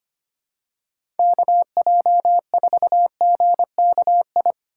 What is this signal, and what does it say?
Morse code, heard as a keyed tone, about 25 wpm, 700 Hz. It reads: KJ4GKI
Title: USA Call Signs - 6 Characters - 4x Character Spacing - 25wpm